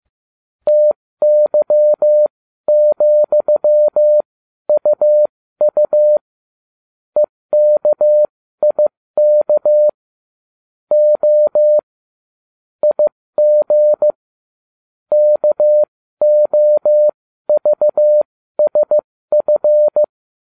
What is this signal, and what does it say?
Morse code, heard as a keyed tone, about 15 wpm, spaced as for 13 wpm, 600 Hz. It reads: TY,UU EKIK O IG KOVSF